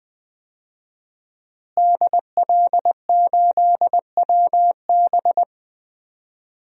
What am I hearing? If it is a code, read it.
Morse code, heard as a keyed tone, about 21 wpm, 700 Hz. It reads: DL8WB
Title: DXCC-100 Call Signs - 5 Characters - Repeated 2x - 20wpm